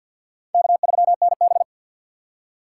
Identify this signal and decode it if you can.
Morse code, heard as a keyed tone, about 46 wpm, 700 Hz. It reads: K3NB
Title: USA Call Signs - 4 Characters - 2x Character Spacing - 50wpm